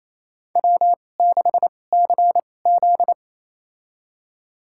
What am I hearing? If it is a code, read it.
Morse code, heard as a keyed tone, about 28 wpm, spaced as for 20 wpm, 700 Hz. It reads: W6CZ
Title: USA Call Signs - 4 Characters - 2x Character Spacing - 28wpm